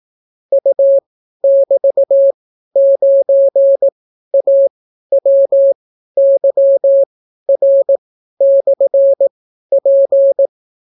Morse code, heard as a keyed tone, about 18 wpm, spaced as for 12 wpm, 550 Hz. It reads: U<BT>9AWYR/P